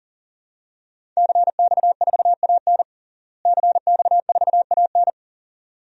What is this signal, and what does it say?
Morse code, heard as a keyed tone, about 40 wpm, 700 Hz. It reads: CX4AN CX4AN
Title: DXCC-100 Call Signs - 5 Characters - Repeated 2x - 40wpm